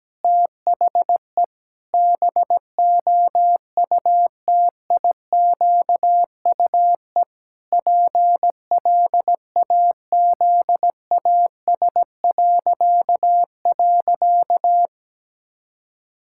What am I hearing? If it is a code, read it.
Morse code, heard as a keyed tone, about 17 wpm, 700 Hz. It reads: THE BOUTIQUE PLAZAS..
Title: Pangram Challenge v2 - 17wpm